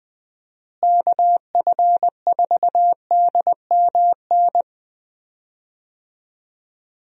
Morse code, heard as a keyed tone, about 21 wpm, 700 Hz. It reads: KF4DMN
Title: USA Call Signs - 6 Characters - Repeated 3x - 20wpm